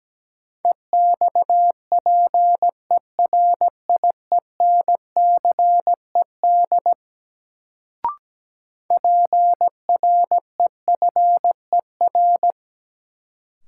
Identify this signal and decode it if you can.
Morse code, heard as a keyed tone, about 17 wpm, 700 Hz. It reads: EXPERIENCED PREFER
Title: Sentences from Top 1100 Words - New Word Drill - 17wpm